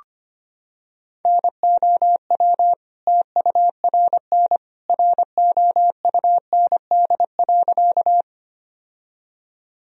Morse code, heard as a keyed tone, about 25 wpm, 700 Hz. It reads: NOW TURN ROUND.
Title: Sentences from Top 1000 Words - Review Part 1 - 25wpm